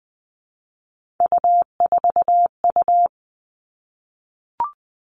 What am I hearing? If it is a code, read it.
Morse code, heard as a keyed tone, about 20 wpm, 700 Hz. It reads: U4U
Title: Three Challenging Letter-Number Combinations 20wpm